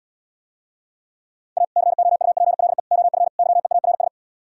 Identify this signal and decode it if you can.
Morse code, heard as a keyed tone, about 77 wpm, 700 Hz. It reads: A COUPLE OF YEARS